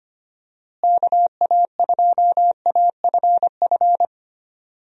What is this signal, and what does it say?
Morse code, heard as a keyed tone, about 25 wpm, 700 Hz. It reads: KA2AFF
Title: USA Call Signs - 6 Characters - Repeated 3x - 25wpm